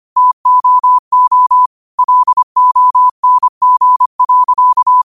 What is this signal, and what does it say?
Morse code, heard as a keyed tone, about 25 wpm, 1000 Hz. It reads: TOO LONG.